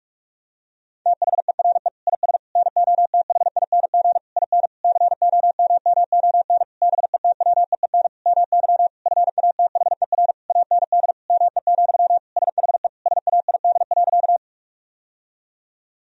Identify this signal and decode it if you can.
Morse code, heard as a keyed tone, about 46 wpm, 700 Hz. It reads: THERE IS NOTHING IN COMMON BETWEEN MY FATHER AND ME, SHE SAID.
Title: Sentences from Top 1000 Words - Review Part 3 - 45wpm